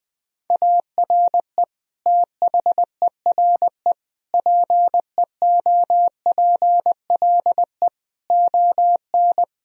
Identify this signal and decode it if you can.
Morse code, heard as a keyed tone, about 20 wpm, 700 Hz. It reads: ARE THERE PEOPLE ON
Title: Sentences from Top 1000 Words - Review Part 3 - 20wpm